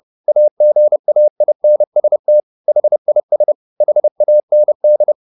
Morse code, heard as a keyed tone, about 30 wpm, 600 Hz. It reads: AGAINST HIS HAND
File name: the old man and the sea / the old man and the sea 1 30wpm